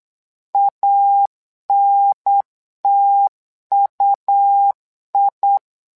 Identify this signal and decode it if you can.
Morse code, heard as a keyed tone, about 8 wpm, 800 Hz. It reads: ANTUI
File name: LEZIONI / 2 UHDN / 2 UHDN 8wpm2232